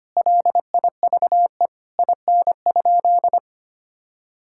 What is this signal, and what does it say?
Morse code, heard as a keyed tone, about 25 wpm, 700 Hz. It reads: LIVE IN?
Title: Sentences from Top 1000 Words - Review Part 5 - 25wpm